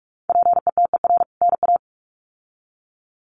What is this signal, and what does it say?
Morse code, heard as a keyed tone, about 44 wpm, 700 Hz. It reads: PETER NA
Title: CWT Exchange 45wpm